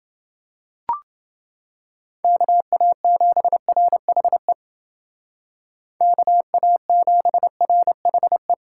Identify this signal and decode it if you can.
Morse code, heard as a keyed tone, about 29 wpm, 700 Hz. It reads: KA7RHE KA7RHE